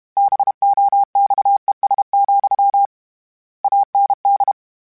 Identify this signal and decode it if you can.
Morse code, heard as a keyed tone, about 32 wpm, 800 Hz. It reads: BOXES, AND